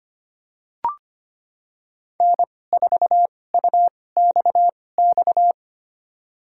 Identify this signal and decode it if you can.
Morse code, heard as a keyed tone, about 25 wpm, spaced as for 18 wpm, 700 Hz. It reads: N4UXX
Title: USA Call Signs - 5 Characters - 2x Character Spacing - 25wpm